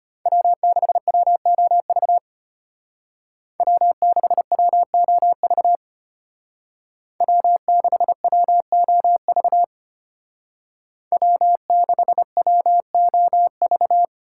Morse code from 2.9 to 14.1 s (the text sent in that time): W6WOV W6WOV W6WOV